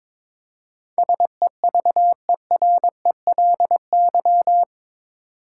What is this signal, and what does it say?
Morse code, heard as a keyed tone, about 22 wpm, 700 Hz. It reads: SEVERELY
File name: practice / 22wpm / Eight Letter Words - 22wpm